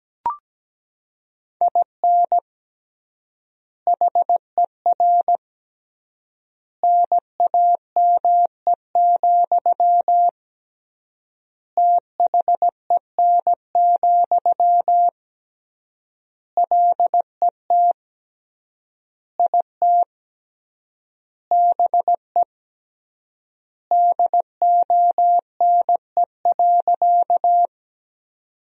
Morse code, heard as a keyed tone, about 17 wpm, 700 Hz. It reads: IN HER NAME, THEN, LET IT BE DONE.
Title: 3x Word-Spacing - Sentences from Top 500 Words - 17wpm